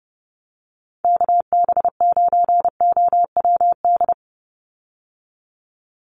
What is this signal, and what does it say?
Morse code, heard as a keyed tone, about 30 wpm, 700 Hz. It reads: KB9OWD
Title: Call Signs - 30wpm